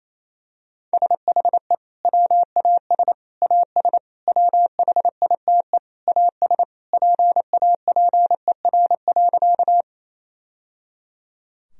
Morse code, heard as a keyed tone, about 28 wpm, 700 Hz. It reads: SHE WAS AS WHITE AS PAPER.
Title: Sentences from Top 1000 Words - Review Part 1 - 28wpm